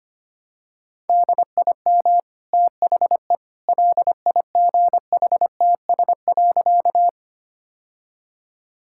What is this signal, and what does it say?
Morse code, heard as a keyed tone, about 25 wpm, 700 Hz. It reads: DIM THE LIGHTS.
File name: Easy Sentences with Hints - 25wpm